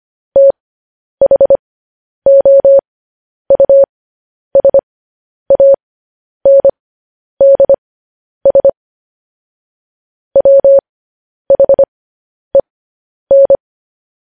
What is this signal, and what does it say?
Morse code, heard as a keyed tone, about 25 wpm, spaced as for 10 wpm, 550 Hz. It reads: THOUSANDS WHEN
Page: news headlines in morse code at 10 wpm